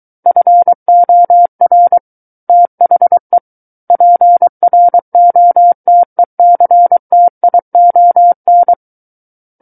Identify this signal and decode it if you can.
Morse code, heard as a keyed tone, about 23 wpm, 700 Hz. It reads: FOR THE PROTECTION
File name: Book One / 14-Chapter 14 - 23 WPM